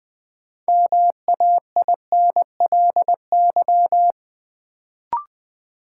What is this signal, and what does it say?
Morse code, heard as a keyed tone, about 20 wpm, 700 Hz. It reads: MAINLY
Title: Sentences from Top 1400 Words - New Word Drill - 20wpm